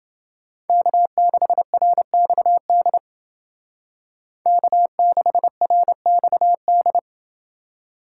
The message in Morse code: K6RXD K6RXD

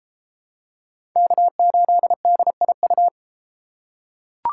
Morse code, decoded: K8DIU

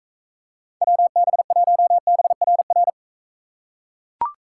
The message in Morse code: WB1BRR

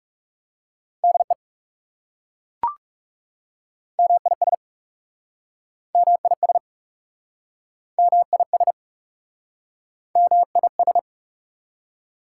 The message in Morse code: DE MIS MIS MIS MIS

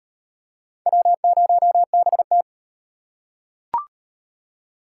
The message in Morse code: W0BT

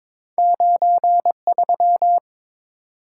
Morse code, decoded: 93

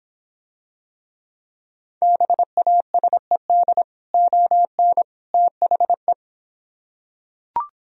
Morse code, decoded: BASED ON THE

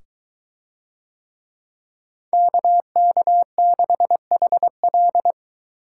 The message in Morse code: KK6HL

